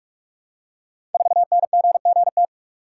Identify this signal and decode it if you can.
Morse code, heard as a keyed tone, about 44 wpm, 700 Hz. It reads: 4NGGT
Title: Code Groups - 5 Letter:Numbers - 45wpm